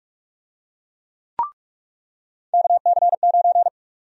Morse code, heard as a keyed tone, about 42 wpm, 700 Hz. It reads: KC9
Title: USA Call Sign Prefixes - 45wpm